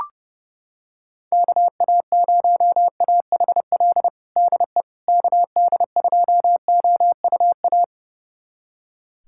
Morse code, heard as a keyed tone, about 28 wpm, 700 Hz. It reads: KA0AHL DE KD2OUA
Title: USA Callsign DE Callsign 30wpm